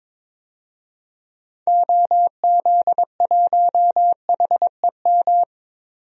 Morse code, heard as a keyed tone, about 22 wpm, 700 Hz. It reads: OZ1HEM
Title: DXCC-100 Call Signs - 6 Characters - 22wpm